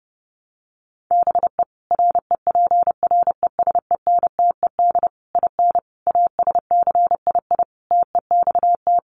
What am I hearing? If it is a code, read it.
Morse code, heard as a keyed tone, about 30 wpm, 700 Hz. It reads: BE REPRESENTED IN ASCII TEXT
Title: Sets of 5 Words 30wpm